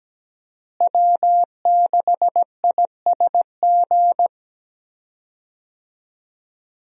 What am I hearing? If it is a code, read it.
Morse code, heard as a keyed tone, about 17 wpm, 700 Hz. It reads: W6ISG
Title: USA Call Signs - 5 Characters - Repeated 2x - 17wpm